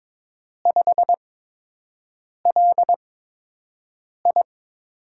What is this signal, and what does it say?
Morse code, heard as a keyed tone, about 22 wpm, spaced as for 6 wpm, 700 Hz. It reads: 5LI
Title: USA Call Signs - 4 Characters - 8x Character Spacing - 22wpm